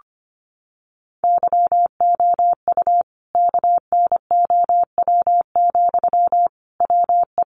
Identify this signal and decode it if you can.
Morse code, heard as a keyed tone, about 25 wpm, 700 Hz. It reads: YOU KNOW, WE
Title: Sentences from Top 600 Words 25wpm